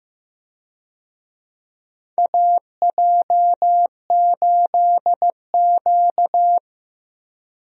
Morse code, heard as a keyed tone, about 15 wpm, 700 Hz. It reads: AJ8Q
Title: USA Call Signs - 4 Characters - Repeated 3x - 15wpm